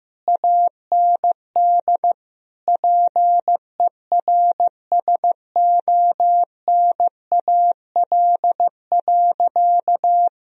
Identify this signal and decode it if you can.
Morse code, heard as a keyed tone, about 16 wpm, 700 Hz. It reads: AND PERSONAL.
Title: Sentences from Top 1000 Words - Review Part 4 - 15wpm